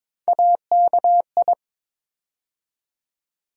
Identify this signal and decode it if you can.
Morse code, heard as a keyed tone, about 22 wpm, 700 Hz. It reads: AKI
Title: USA Call Sign Postfixes - 3 Letters 22wpm